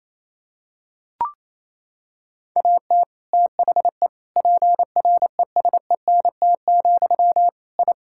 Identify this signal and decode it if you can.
Morse code, heard as a keyed tone, about 28 wpm, 700 Hz. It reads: AT THE PRESENT, I
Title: Sentences from Top 1000 Words - Review Part 5 - 28wpm